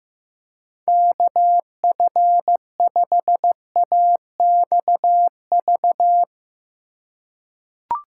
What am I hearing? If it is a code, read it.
Morse code, heard as a keyed tone, about 15 wpm, 700 Hz. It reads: KF5AXV
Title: USA Call Signs - 2x Character Spacing - 15wpm